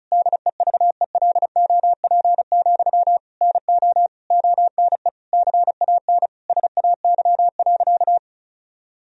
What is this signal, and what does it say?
Morse code, heard as a keyed tone, about 36 wpm, 700 Hz. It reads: DEVELOP, NO ONE CAN SAY.
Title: Sentences from Top 1000 Words - Review Part 2 - 35wpm